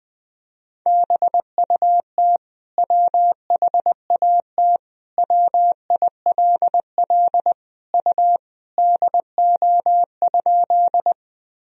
Morse code, heard as a keyed tone, about 20 wpm, 700 Hz. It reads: BUT WHAT WILL U DO?
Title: Sentences from Top 100 Words and Abbreviations 20wpm